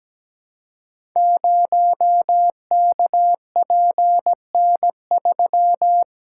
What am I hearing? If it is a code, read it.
Morse code, heard as a keyed tone, about 17 wpm, 700 Hz. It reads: 0KPN3
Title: Code Groups - 5 Letter:Numbers - 17wpm